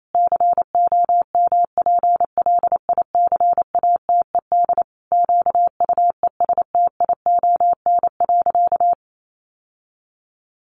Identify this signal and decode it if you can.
Morse code, heard as a keyed tone, about 28 wpm, 700 Hz. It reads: COMPLICATED QUESTION.
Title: Sentences from Top 1200 Words 28wpm